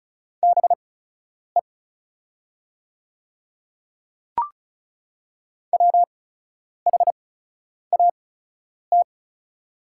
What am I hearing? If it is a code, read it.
Morse code, heard as a keyed tone, about 34 wpm, spaced as for 10 wpm, 700 Hz. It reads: BE WHAT
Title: 8x Character Spacing - Top 100 Words - 35wpm